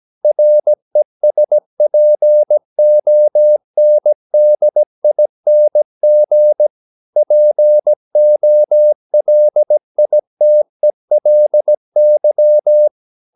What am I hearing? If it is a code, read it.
Morse code, heard as a keyed tone, about 17 wpm, 600 Hz. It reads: RESPONDING POLITELY
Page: Romanen Never af Ken Follet på engelsk.